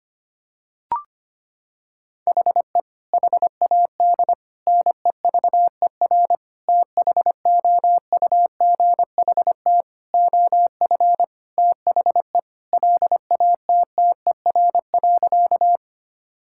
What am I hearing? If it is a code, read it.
Morse code, heard as a keyed tone, about 25 wpm, 700 Hz. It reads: HE HAD NEVER THOUGHT OF THE LATTER.
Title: Sentences from Top 1600 Words 25wpm